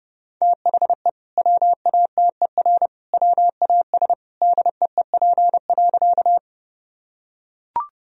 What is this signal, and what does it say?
Morse code, heard as a keyed tone, about 32 wpm, 700 Hz. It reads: THE WATER WAS DEEP.
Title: Sentences from Top 1000 Words - Review Part 3 - 30wpm